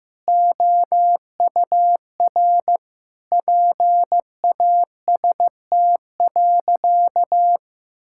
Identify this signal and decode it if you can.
Morse code, heard as a keyed tone, about 15 wpm, 700 Hz. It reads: OUR PAST.
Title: Sentences from Top 1000 Words - Review Part 6 - 15wpm